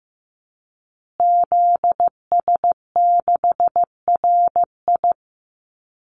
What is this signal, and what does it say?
Morse code, heard as a keyed tone, about 15 wpm, 700 Hz. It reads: ZS6RI
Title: Call Signs Repeated 3x - 15wpm